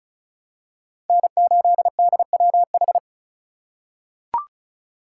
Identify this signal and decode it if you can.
Morse code, heard as a keyed tone, about 36 wpm, 700 Hz. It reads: N8DWH
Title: USA Call Signs - 4x Character Spacing - 35wpm